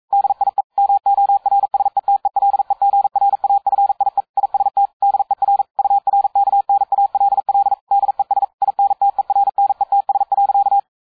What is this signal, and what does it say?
Morse code, heard as a keyed tone, about 42 wpm, 800 Hz. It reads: DIE MORSETELEGRAFIE IST DER URKNALL DES INTERNETS.